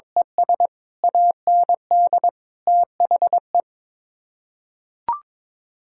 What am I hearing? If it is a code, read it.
Morse code, heard as a keyed tone, about 21 wpm, 700 Hz. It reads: ES AND THE
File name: practice / 22wpm / Build A Phrase - 5-Word Phrases - 22wpm